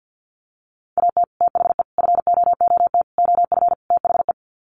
Morse code, heard as a keyed tone, about 50 wpm, 700 Hz. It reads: AT THE FOOT OF THE
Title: Sets of 5 Words 50wpm